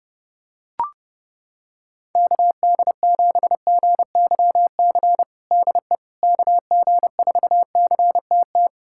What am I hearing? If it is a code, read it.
Morse code, heard as a keyed tone, about 32 wpm, 700 Hz. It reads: KD7GYC DE KG4CTT
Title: USA Callsign DE Callsign - 6 Characters - 30wpm